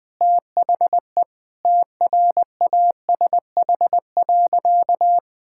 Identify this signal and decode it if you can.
Morse code, heard as a keyed tone, about 20 wpm, 700 Hz. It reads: THE TRASH.
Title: Easy Sentences with Hints - 20wpm